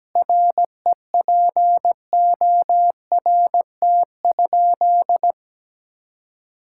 Morse code, heard as a keyed tone, about 17 wpm, 700 Hz. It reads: REPORT?
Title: Sentences from Top 1000 Words - Review Part 6 - 17wpm